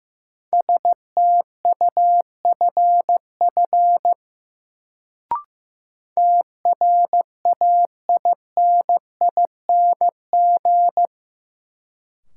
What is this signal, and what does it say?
Morse code, heard as a keyed tone, about 15 wpm, 700 Hz. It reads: STUFF TRAINING
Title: Sentences from Top 1400 Words - New Word Drill Alt - 15wpm